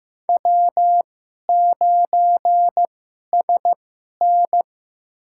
Morse code, heard as a keyed tone, about 15 wpm, spaced as for 11 wpm, 700 Hz. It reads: W9SN
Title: USA Call Signs - 4 Characters - 2x Character Spacing - 15wpm